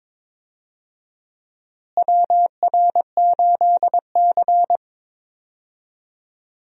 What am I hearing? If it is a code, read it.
Morse code, heard as a keyed tone, about 22 wpm, 700 Hz. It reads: WR8C
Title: USA Call Signs - 4 Characters - Repeated 2x - 22wpm